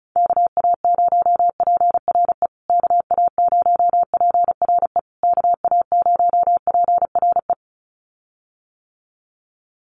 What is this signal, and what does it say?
Morse code, heard as a keyed tone, about 34 wpm, 700 Hz. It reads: KA0PRE KA0PRE KA0PRE
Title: USA Call Signs Repeated 3x 35wpm